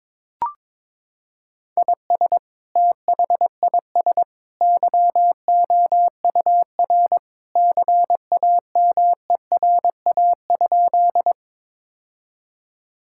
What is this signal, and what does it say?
Morse code, heard as a keyed tone, about 21 wpm, 700 Hz. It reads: IS THIS YOUR CAMERA?
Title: Sentences from Top 1000 Words - Review Part 5 - 22wpm